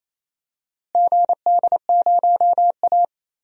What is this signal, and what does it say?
Morse code, heard as a keyed tone, about 28 wpm, 700 Hz. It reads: GD0A
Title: DXCC-100 Call Signs - 4 Characters - Repeated 3x - 28wpm